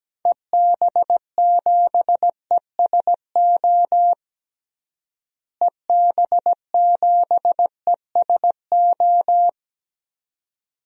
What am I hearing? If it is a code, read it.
Morse code, heard as a keyed tone, about 17 wpm, 700 Hz. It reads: EB7ESO EB7ESO
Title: DXCC-100 Call Signs - 6 Characters - Repeated 2x - 17wpm